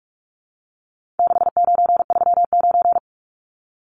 Morse code, heard as a keyed tone, about 44 wpm, 700 Hz. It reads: 6939
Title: Numbers - 4-Digit 45wpm